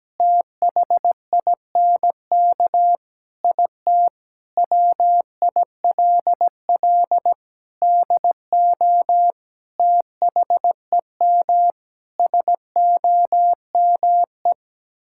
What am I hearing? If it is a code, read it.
Morse code, heard as a keyed tone, about 17 wpm, 700 Hz. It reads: THINK IT WILL DO THEM SOME
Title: Sentences from Top 1000 Words - Review Part 6 - 17wpm